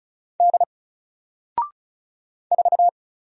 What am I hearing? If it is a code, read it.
Morse code, heard as a keyed tone, about 36 wpm, 700 Hz. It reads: D 4
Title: Single Character (Letter:Number:Punctuation:Prosign) 35wpm